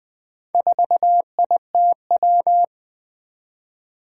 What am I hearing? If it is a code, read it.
Morse code, heard as a keyed tone, about 20 wpm, 700 Hz. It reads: 4ITW
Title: Four Challenging Letter-Number Combinations 20wpm